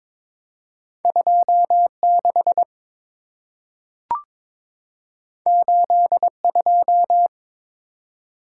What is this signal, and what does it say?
Morse code, heard as a keyed tone, about 22 wpm, 700 Hz. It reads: 26 82
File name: practice / 22wpm / Two Letter-Number Combinations - 22wpm